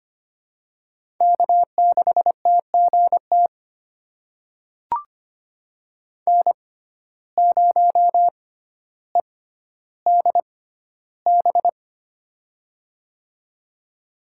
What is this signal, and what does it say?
Morse code, heard as a keyed tone, about 25 wpm, spaced as for 13 wpm, 700 Hz. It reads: K6TGT N0EDB
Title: USA Call Signs - 5 Characters - 6x Character Spacing - 25wpm